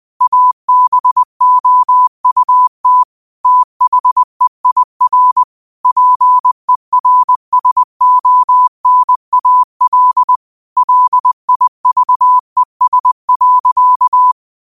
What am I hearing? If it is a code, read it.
Morse code, heard as a keyed tone, about 20 wpm, 1000 Hz. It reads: ABOUT THEIR PERSONAL LIVES.
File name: qotdfiles / 20wpm